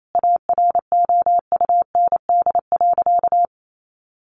Morse code, heard as a keyed tone, about 28 wpm, 700 Hz. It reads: AROUND.
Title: Sentences from Top 800 Words 28wpm